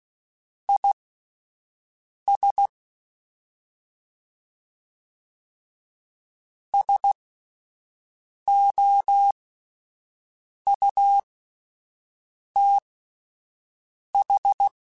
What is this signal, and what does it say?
Morse code, heard as a keyed tone, about 16 wpm, spaced as for 5 wpm, 800 Hz. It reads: IS SOUTH
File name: (spaced) / QSOs Spaced-mp3s / kc7haa @10 sp